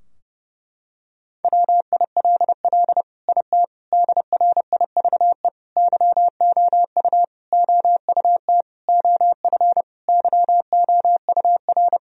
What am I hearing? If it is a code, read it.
Morse code, heard as a keyed tone, about 30 wpm, 700 Hz. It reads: WILL IT DRIVE YOU OUT OF YOUR
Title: Sentences from Top 1000 Words - Review Part 2 - 30wpm